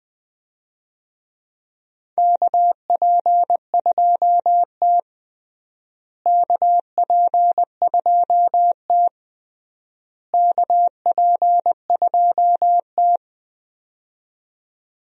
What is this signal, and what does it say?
Morse code, heard as a keyed tone, about 21 wpm, 700 Hz. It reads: KP2T KP2T KP2T
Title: DXCC-100 Call Signs - 4 Characters - Repeated 3x - 20wpm